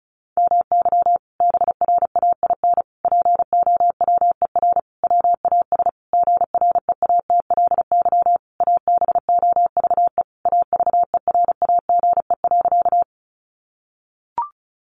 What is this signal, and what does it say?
Morse code, heard as a keyed tone, about 34 wpm, 700 Hz. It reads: MY BRAIN POWER WAS GREATLY ABOVE AVERAGE.
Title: Sentences from Top 1400 Words 35wpm